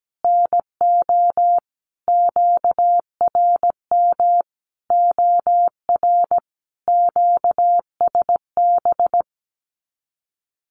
Morse code, heard as a keyed tone, about 17 wpm, 700 Hz. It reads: NO QRM OR QSB
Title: Common QSO Phrases 17wpm